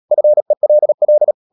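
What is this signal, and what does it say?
Morse code, heard as a keyed tone, about 37 wpm, 600 Hz. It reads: FELL
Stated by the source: Morse code practice at 35–39 wpm sending English text of Ken Follett's novel Never, from CW med Gnister.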